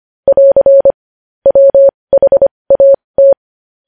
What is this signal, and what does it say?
Morse code, heard as a keyed tone, about 25 wpm, spaced as for 20 wpm, 550 Hz. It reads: <AR> WHAT
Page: news-headlines-in-morse-code-at-20-wpm